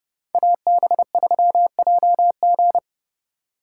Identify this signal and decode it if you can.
Morse code, heard as a keyed tone, about 37 wpm, 700 Hz. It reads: AB3JG
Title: Speed Racing - USA 2x2 Call Signs 30wpm